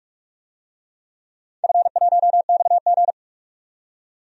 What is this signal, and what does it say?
Morse code, heard as a keyed tone, about 42 wpm, 700 Hz. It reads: F1XG
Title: DXCC-100 Call Signs - 4 Characters - 45wpm